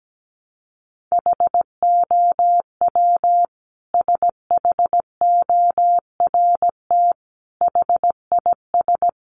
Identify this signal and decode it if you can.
Morse code, heard as a keyed tone, about 17 wpm, 700 Hz. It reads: HOW SHORT HIS
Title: Sentences from Top 400 Words 17wpm